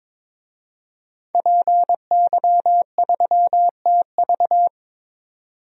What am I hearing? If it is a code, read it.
Morse code, heard as a keyed tone, about 22 wpm, 700 Hz. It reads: PY3TV
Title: DXCC-100 Call Signs - 5 Characters - 22wpm